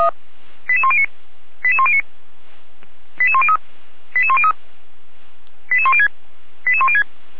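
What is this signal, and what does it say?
Five-tone selective call (ZVEI), heard as quick runs of five tones, tones of about 70 ms. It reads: 88109 88109 88103 88103 88106 88106
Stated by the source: ZVEI five-tone fire brigade alert received with an SDR by Feuerwehr Orlamünde.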